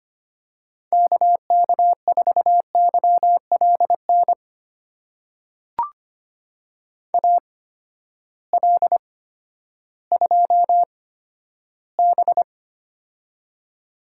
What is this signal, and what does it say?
Morse code, heard as a keyed tone, about 25 wpm, spaced as for 12 wpm, 700 Hz. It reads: KK4YLN AL2B
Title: USA Call Signs - 8x Character Spacing - 25wpm